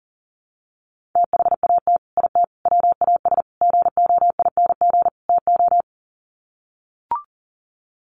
Morse code, heard as a keyed tone, about 40 wpm, 700 Hz. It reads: THAT IT WAS GOING TO
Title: Sets of 5 Words Encore2 40wpm